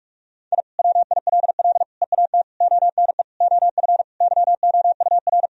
Morse code, heard as a keyed tone, about 46 wpm, 700 Hz. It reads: I WILL EAT ONE OF YOUR
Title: Sentences from Top 1000 Words - Review Part 1 - 45wpm